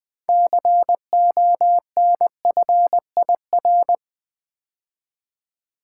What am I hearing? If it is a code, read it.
Morse code, heard as a keyed tone, about 20 wpm, 700 Hz. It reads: CONFIR
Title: Build A Word - 6 Letters At A Time - 12-Letter Words - 20wpm